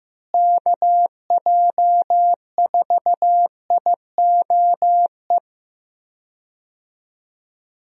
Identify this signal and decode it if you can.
Morse code, heard as a keyed tone, about 15 wpm, 700 Hz. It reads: KJ4IOE
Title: USA Call Signs - 6 Characters - Repeated 2x - 15wpm